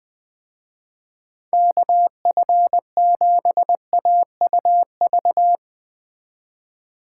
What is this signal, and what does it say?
Morse code, heard as a keyed tone, about 20 wpm, 700 Hz. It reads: KF7AUV
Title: Speed Racing - USA 2x3 Call Signs 17wpm